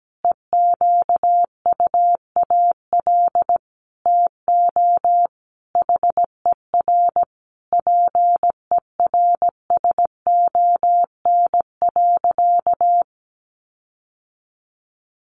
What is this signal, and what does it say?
Morse code, heard as a keyed tone, about 17 wpm, 700 Hz. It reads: EQUAL TO HER PERSON.
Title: Sentences from Top 1400 Words 17wpm